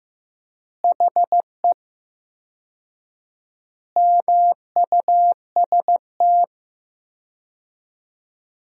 Morse code, heard as a keyed tone, about 14 wpm, 700 Hz. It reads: HE MUST
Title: 4x Word-Spacing - Sentences from Top 500 Words - 15wpm